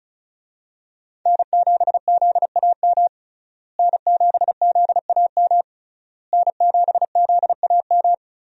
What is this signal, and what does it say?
Morse code, heard as a keyed tone, about 36 wpm, 700 Hz. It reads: N7ZAM N7ZAM N7ZAM
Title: USA Call Signs - 5 Characters - Repeated 3x - 35wpm